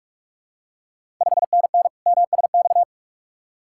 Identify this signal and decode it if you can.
Morse code, heard as a keyed tone, about 42 wpm, 690 Hz. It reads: 5NN MSX
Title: Ontario QSO Party Exchange 45wpm